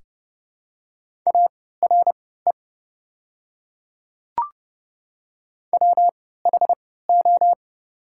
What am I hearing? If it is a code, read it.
Morse code, heard as a keyed tone, about 30 wpm, spaced as for 17 wpm, 690 Hz. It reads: ARE WHO
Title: 3x Character Spacing - Top 200 Words - 30wpm